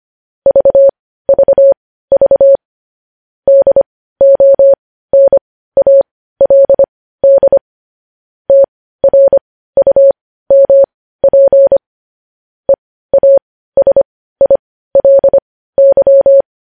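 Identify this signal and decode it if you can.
Morse code, heard as a keyed tone, about 25 wpm, spaced as for 15 wpm, 550 Hz. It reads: VVV DONALD TRUMP EASILY
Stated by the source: News headlines sent as Morse practice at 15.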